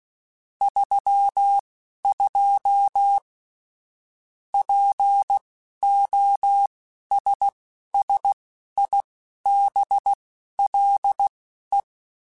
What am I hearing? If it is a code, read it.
Morse code, heard as a keyed tone, about 16 wpm, spaced as for 11 wpm, 800 Hz. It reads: 32 POSSIBLE
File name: Technical Subjects / Technical Subjects-mp3s / AND Gate @10